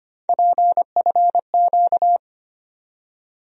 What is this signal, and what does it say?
Morse code, heard as a keyed tone, about 25 wpm, 700 Hz. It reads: PFQ
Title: USA Call Sign Postfixes - 3 Letters 25wpm